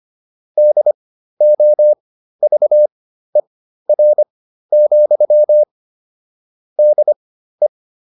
Morse code, heard as a keyed tone, about 25 wpm, spaced as for 13 wpm, 600 Hz. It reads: DOVER, DE